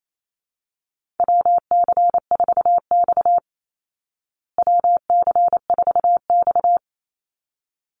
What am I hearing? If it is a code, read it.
Morse code, heard as a keyed tone, about 28 wpm, 700 Hz. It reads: WC4X WC4X
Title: Call Signs Repeated 3x - 28wpm